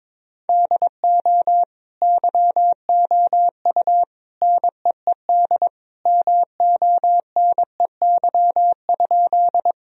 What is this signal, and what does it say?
Morse code, heard as a keyed tone, about 22 wpm, 700 Hz. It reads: DO YOU NEED MONEY?
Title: Sentences from Top 1000 Words - Review Part 1 - 22wpm